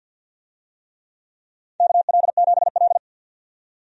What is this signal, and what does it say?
Morse code, heard as a keyed tone, about 53 wpm, 700 Hz. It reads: KL7L
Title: DXCC-100 Call Signs - 4 Characters - 50wpm